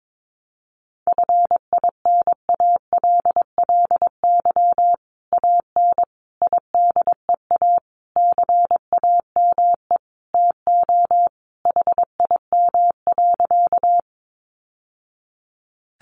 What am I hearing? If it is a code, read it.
Morse code, heard as a keyed tone, about 22 wpm, 700 Hz. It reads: FINALLY AN IDEA CAME TO HIM.
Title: Sentences from Top 1000 Words 22wpm